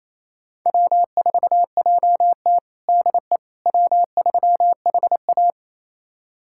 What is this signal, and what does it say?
Morse code, heard as a keyed tone, about 28 wpm, 700 Hz. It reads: W4JT DE W3HA
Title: USA Callsign DE Callsign - 4 Characters - 28wpm